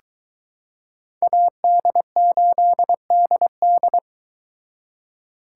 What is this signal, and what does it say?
Morse code, heard as a keyed tone, about 22 wpm, 700 Hz. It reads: AD8DD